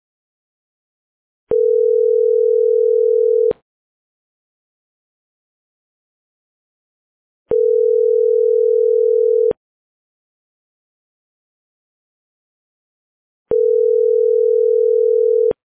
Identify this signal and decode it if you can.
Ringback tone (North America), 440 + 480 Hz.